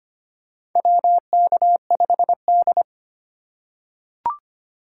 Morse code, heard as a keyed tone, about 25 wpm, 700 Hz. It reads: WK5D
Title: USA Call Signs - 4 Characters - 3x Character Spacing - 25wpm